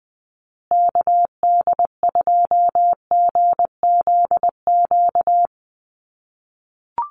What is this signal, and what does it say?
Morse code, heard as a keyed tone, about 19 wpm, 700 Hz. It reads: KD2GZQ
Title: USA Call Signs Repeated 3x 20wpm